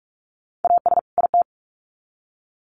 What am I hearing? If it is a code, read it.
Morse code, heard as a keyed tone, about 44 wpm, 700 Hz. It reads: AS IT
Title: Sets of 2 Words 45wpm